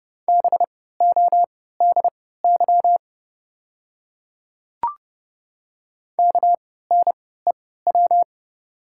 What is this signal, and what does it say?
Morse code, heard as a keyed tone, about 32 wpm, spaced as for 17 wpm, 700 Hz. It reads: BODY KNEW
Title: 3x Character Spacing - Top 500 Words - 30wpm